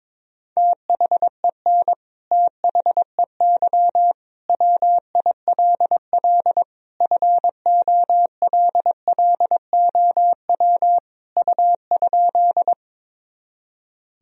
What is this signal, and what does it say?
Morse code, heard as a keyed tone, about 22 wpm, 700 Hz. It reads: THEN THEY WILL FOLLOW U?
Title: Sentences from Top 200 Words and Abbreviations 22wpm